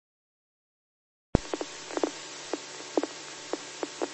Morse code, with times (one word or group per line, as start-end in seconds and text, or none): none